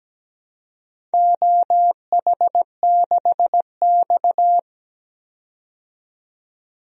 Morse code, heard as a keyed tone, about 17 wpm, 700 Hz. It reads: OH6X
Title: DXCC-100 Call Signs - 4 Characters - 17wpm